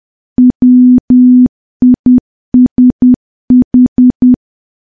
Morse code, heard as a keyed tone, about 10 wpm, 250 Hz. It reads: WISH
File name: morse / morse-training-10WPM